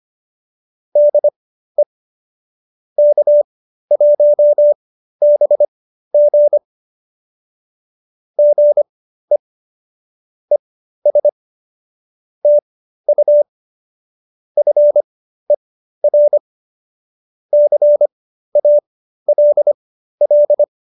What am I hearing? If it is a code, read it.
Morse code, heard as a keyed tone, about 25 wpm, spaced as for 13 wpm, 590 Hz. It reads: DE K1BG GE ES TU FER CALL